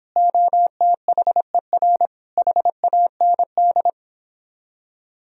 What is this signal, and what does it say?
Morse code, heard as a keyed tone, about 29 wpm, 700 Hz. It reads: OTHER HAND